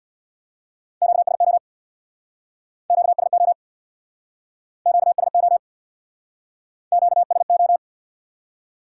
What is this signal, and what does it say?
Morse code, heard as a keyed tone, about 66 wpm, 700 Hz. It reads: QSO QSO QSO QSO